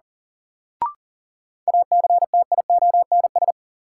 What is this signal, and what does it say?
Morse code, heard as a keyed tone, about 37 wpm, 700 Hz. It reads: ACTIONS
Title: Sentences from Top 1000 Words - Review Part 6 - New Word Drill - 40wpm